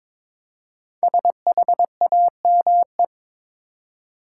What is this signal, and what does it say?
Morse code, heard as a keyed tone, about 22 wpm, 700 Hz. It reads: SHAME
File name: 5 Letter Words - First 3 Letters Given - 22wpm